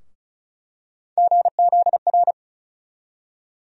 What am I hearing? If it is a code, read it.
Morse code, heard as a keyed tone, about 34 wpm, 700 Hz. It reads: GZR
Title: USA Call Sign Postfixes - 3 Letters 35wpm